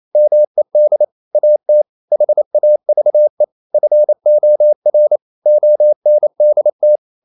Morse code, heard as a keyed tone, about 28 wpm, 600 Hz. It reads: MED AT HAVE FOR ONDT
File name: Ferie QRQ 28wpm